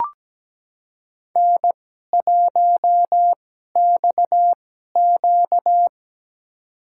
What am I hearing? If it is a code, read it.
Morse code, heard as a keyed tone, about 17 wpm, spaced as for 12 wpm, 700 Hz. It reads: N1XQ
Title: USA Call Signs - 4 Characters - 2x Character Spacing - 17wpm